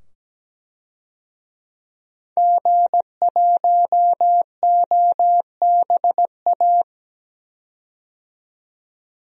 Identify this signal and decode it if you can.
Morse code, heard as a keyed tone, about 17 wpm, 700 Hz. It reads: G1OBA
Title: DXCC-100 Call Signs - 5 Characters - 17wpm